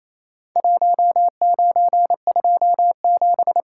Morse code, heard as a keyed tone, about 27 wpm, 700 Hz. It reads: 1927
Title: Worked All Europe DX Contest - QTC Exchange - 28wpm